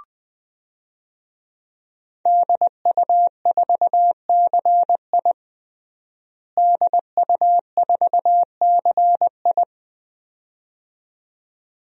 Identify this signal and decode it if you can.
Morse code, heard as a keyed tone, about 20 wpm, 700 Hz. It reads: DU4CI DU4CI
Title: DXCC-100 Call Signs - 5 Characters - Repeated 2x - 20wpm